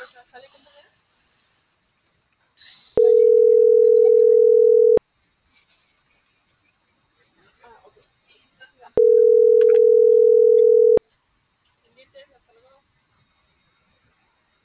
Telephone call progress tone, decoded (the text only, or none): ringback tone